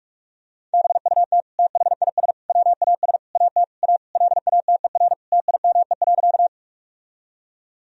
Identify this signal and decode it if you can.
Morse code, heard as a keyed tone, about 41 wpm, 700 Hz. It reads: BUT THIS WAS AT A LATER TIME.
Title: Sentences from Top 1000 Words - Review Part 4 - 45wpm